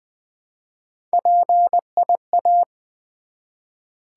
Morse code, heard as a keyed tone, about 20 wpm, 700 Hz. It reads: PIA
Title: Build A Word - 1 Letter At A Time - 5-Letter Words - 20wpm